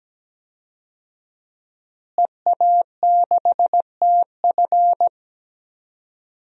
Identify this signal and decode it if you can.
Morse code, heard as a keyed tone, about 17 wpm, 700 Hz. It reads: EA6TF
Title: DXCC-100 Call Signs - 5 Characters - Repeated 3x - 17wpm